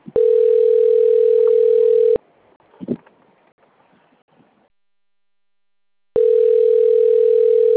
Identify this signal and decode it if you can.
Ringback tone (North America), 440 + 480 Hz.